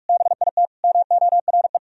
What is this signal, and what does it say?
Morse code, heard as a keyed tone, about 58 wpm, 700 Hz. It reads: BIT MORE